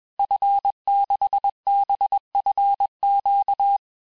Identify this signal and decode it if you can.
Morse code, heard as a keyed tone, about 21 wpm, 800 Hz. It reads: F6BFQ
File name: teste-FIN-de-QSO-vitesse-20mpm